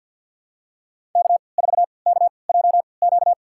Morse code, heard as a keyed tone, about 51 wpm, spaced as for 28 wpm, 690 Hz. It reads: K4KJQ
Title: USA Call Signs - 5 Characters - 3x Character Spacing - 50wpm